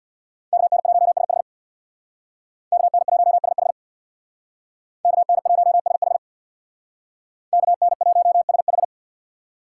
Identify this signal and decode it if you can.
Morse code, heard as a keyed tone, about 62 wpm, 700 Hz. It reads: KN1SH KN1SH KN1SH KN1SH
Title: Speed Racing - USA 2x2 Call Signs 50wpm